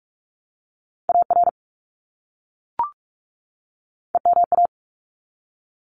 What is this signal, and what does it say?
Morse code, heard as a keyed tone, about 44 wpm, 700 Hz. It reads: AR EMA
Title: ARRL Sections 45wpm